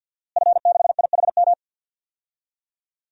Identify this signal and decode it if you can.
Morse code, heard as a keyed tone, about 53 wpm, 700 Hz. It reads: F6IHM
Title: DXCC-100 Call Signs - 5 Characters - Repeated 3x - 50wpm